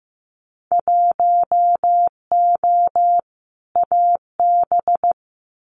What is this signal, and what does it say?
Morse code, heard as a keyed tone, about 15 wpm, 700 Hz. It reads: 1O AB
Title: Winter Field Day Exchanges Encore 15wpm